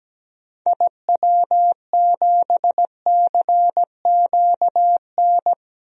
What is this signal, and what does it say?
Morse code, heard as a keyed tone, about 17 wpm, 700 Hz. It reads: IW7CQN
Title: DXCC-100 Call Signs - 6 Characters - 17wpm